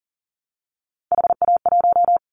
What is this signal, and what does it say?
Morse code, heard as a keyed tone, about 42 wpm, 700 Hz. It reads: HA1
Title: Callsign Prefixes for Top 50 DXCC - 40wpm